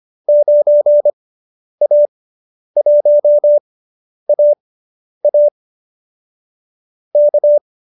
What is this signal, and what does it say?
Morse code, heard as a keyed tone, about 25 wpm, spaced as for 10 wpm, 590 Hz. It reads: 9A1AA K